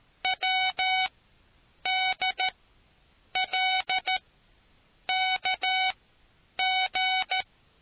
Morse code, heard as a keyed tone, about 13 wpm, spaced as for 8 wpm, 750 Hz. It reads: WDLKG